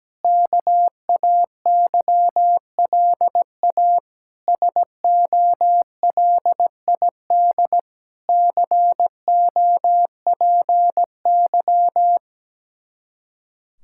Morse code, heard as a keyed tone, about 17 wpm, 700 Hz. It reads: KAYLA SOLID COPY
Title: K1USN SST QSO Practice 17wpm